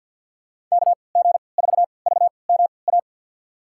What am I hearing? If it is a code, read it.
Morse code, heard as a keyed tone, about 46 wpm, spaced as for 27 wpm, 700 Hz. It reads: KG4VMA
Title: USA Call Signs - 6 Characters - 3x Character Spacing - 50wpm